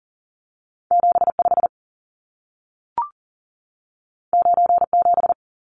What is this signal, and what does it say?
Morse code, heard as a keyed tone, about 40 wpm, 700 Hz. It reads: 75 97
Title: Numbers - 2-Digit 40wpm